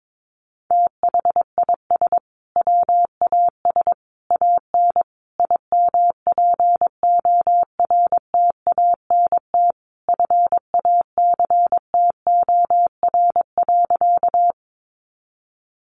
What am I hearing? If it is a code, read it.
Morse code, heard as a keyed tone, about 22 wpm, 700 Hz. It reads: THIS WAS AN IMPORTANT FACTOR.
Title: Sentences from Top 1300 Words 22wpm